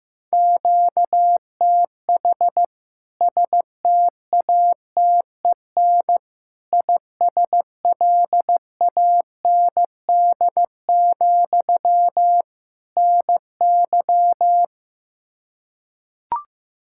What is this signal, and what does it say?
Morse code, heard as a keyed tone, about 16 wpm, 700 Hz. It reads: QTH STATEN ISLAND, NY
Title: K1USN SST QSO Practice 15wpm